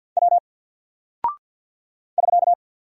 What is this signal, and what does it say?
Morse code, heard as a keyed tone, about 50 wpm, 700 Hz. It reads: W <SK>
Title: Single Character (Letter:Number:Punctuation:Prosign) 50wpm